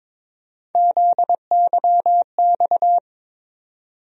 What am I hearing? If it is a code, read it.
Morse code, heard as a keyed tone, about 22 wpm, 700 Hz. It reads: ZYX
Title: USA Call Sign Postfixes - 3 Letters 22wpm